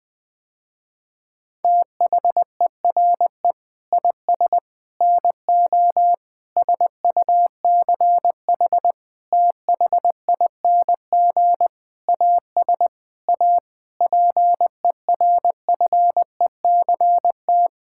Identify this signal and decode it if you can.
Morse code, heard as a keyed tone, about 20 wpm, 700 Hz. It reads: THERE IS NO SUCH THING AS A PERFECT
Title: Sentences with 7-Word Phrases - Spoken Beginning - 20wpm